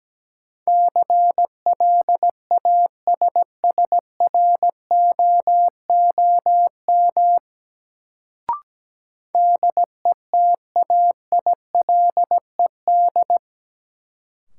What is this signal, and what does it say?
Morse code, heard as a keyed tone, about 17 wpm, 700 Hz. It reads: CLASSROOM DETAILED
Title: Sentences from Top 1500 Words - New Word Drill Alt - 17wpm